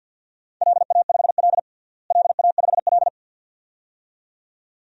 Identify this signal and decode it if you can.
Morse code, heard as a keyed tone, about 51 wpm, 700 Hz. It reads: LA5L LA5L
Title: DXCC-100 Call Signs - 4 Characters - Repeated 2x - 50wpm